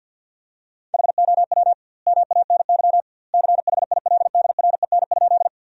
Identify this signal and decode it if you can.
Morse code, heard as a keyed tone, about 48 wpm, 700 Hz. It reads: HOW MANY CHILDREN?
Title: 2x Word-Spacing - Sentences from Top 300 Words - 50wpm